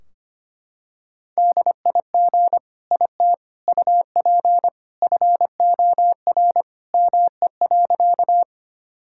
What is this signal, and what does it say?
Morse code, heard as a keyed tone, about 25 wpm, 700 Hz. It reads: DIG IT UP FOR ME.
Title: Sentences from Top 1600 Words 25wpm